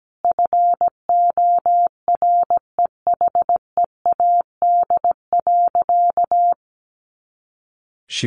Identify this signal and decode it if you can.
Morse code, heard as a keyed tone, about 17 wpm, 700 Hz. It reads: FOREHEAD.
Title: Easy Phrases 17wpm